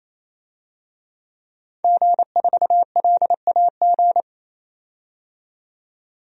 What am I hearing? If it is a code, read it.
Morse code, heard as a keyed tone, about 28 wpm, 700 Hz. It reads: G4LAG
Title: DXCC-100 Call Signs - 5 Characters - 28wpm